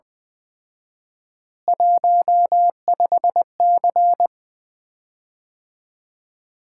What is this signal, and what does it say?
Morse code, heard as a keyed tone, about 20 wpm, 700 Hz. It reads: 15C
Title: Common QSO Phrases - 4x Word-Spacing - 20wpm